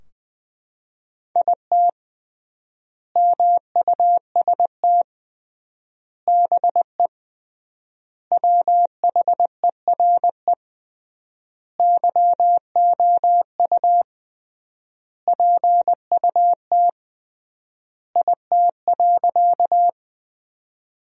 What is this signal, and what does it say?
Morse code, heard as a keyed tone, about 20 wpm, 700 Hz. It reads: IT MUST BE WHERE YOU PUT IT.
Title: 3x Word-Spacing - Sentences from Top 200 Words - 20wpm